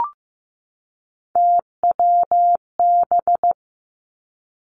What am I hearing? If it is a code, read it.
Morse code, heard as a keyed tone, about 15 wpm, 700 Hz. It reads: TWB
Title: Three Challenging Letter-Number Combinations 15wpm